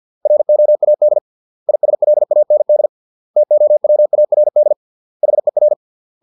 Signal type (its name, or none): Morse code